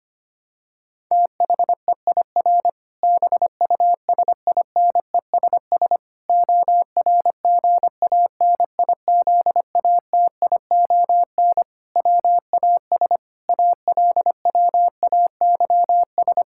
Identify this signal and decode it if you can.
Morse code, heard as a keyed tone, about 25 wpm, 700 Hz. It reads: THEIR BUSINESS ORGANIZATION WAS ALWAYS
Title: Sentences from Top 1000 Words - Review Part 5 - 25wpm